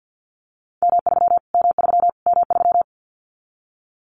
Morse code, heard as a keyed tone, about 49 wpm, 700 Hz. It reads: M3 M3 M3
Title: Callsign Prefixes for Top 25 DXCC - Repeated 3x - 50wpm